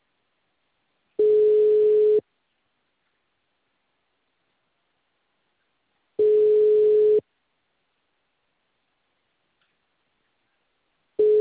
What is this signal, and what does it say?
Ringback tone (Europe, 425 Hz).
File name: out- -2024- - - .17604